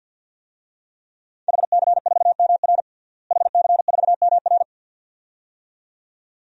Morse code, heard as a keyed tone, about 45 wpm, 700 Hz. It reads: HC4MR HC4MR
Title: DXCC-100 Call Signs - 5 Characters - Repeated 2x - 50wpm